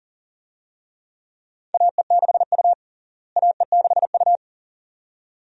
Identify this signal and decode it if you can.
Morse code, heard as a keyed tone, about 40 wpm, 700 Hz. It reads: AE6U AE6U
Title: USA Call Signs - 4 Characters - Repeated 2x - 40wpm